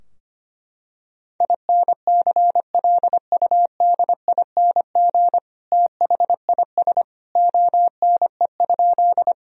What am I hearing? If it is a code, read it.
Morse code, heard as a keyed tone, about 25 wpm, 700 Hz. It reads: INCLUDING THIS ONE?
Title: Sentences from Top 1000 Words - Review Part 4 - 25wpm